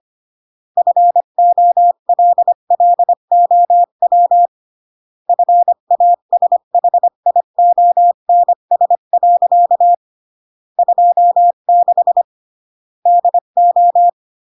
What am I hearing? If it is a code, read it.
Morse code, heard as a keyed tone, about 25 wpm, 700 Hz. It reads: FOLLOW FASHIONS. 26 DO